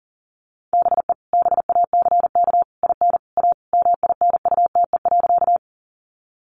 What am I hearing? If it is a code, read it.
Morse code, heard as a keyed tone, about 40 wpm, 700 Hz. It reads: BE BACK IN A MINUTE.
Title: Sentences from Top 500 Words 40wpm